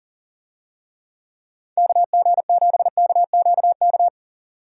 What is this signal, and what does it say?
Morse code, heard as a keyed tone, about 40 wpm, 700 Hz. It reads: KG7KQK
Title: USA Call Signs - 6 Characters - 40wpm